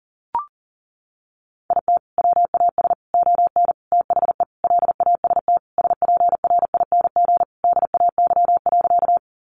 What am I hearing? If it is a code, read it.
Morse code, heard as a keyed tone, about 40 wpm, 700 Hz. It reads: IT WAS ON THE LAST SPRING DAY.
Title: Sentences from Top 1200 Words 40wpm